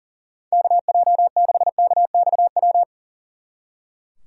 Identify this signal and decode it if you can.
Morse code, heard as a keyed tone, about 38 wpm, 700 Hz. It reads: KJ6KXW
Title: Speed Racing - USA 2x3 Call Signs 40wpm